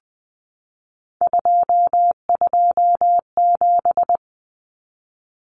Morse code, heard as a keyed tone, about 20 wpm, 700 Hz. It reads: 227
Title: Numbers - 3-Digit 20wpm